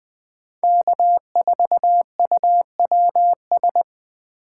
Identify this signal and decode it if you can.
Morse code, heard as a keyed tone, about 20 wpm, 700 Hz. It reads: K4UWS
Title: USA Call Signs - 5 Characters - 4x Character Spacing - 20wpm